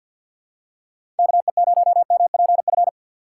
Morse code, heard as a keyed tone, about 52 wpm, 700 Hz. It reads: KE0MPF
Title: USA Call Signs - 4x Character Spacing - 50wpm